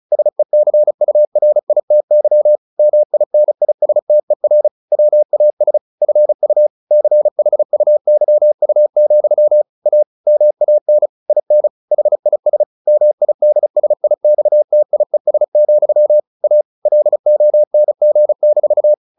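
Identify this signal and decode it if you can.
Morse code, heard as a keyed tone, about 34 wpm, 600 Hz. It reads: SECURITY MINISTER WAS FU CHUYU, A MAN IN HIS MIDSIXTIES, A LONG-
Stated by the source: Morse practice at 35 wpm from CW med Gnister, sending Ken Follett's novel Never in English.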